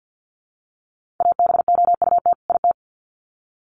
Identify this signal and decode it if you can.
Morse code, heard as a keyed tone, about 50 wpm, 700 Hz. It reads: ABOUT IT
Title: Sets of 2 Words 50wpm